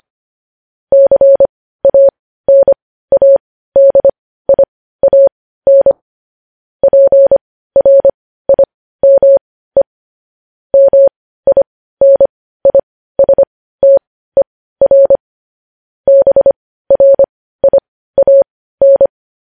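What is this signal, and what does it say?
Morse code, heard as a keyed tone, about 25 wpm, spaced as for 15 wpm, 550 Hz. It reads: CANADIAN PRIME MINISTER BRIAN
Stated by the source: Morse code sending of news headlines at 15 wpm.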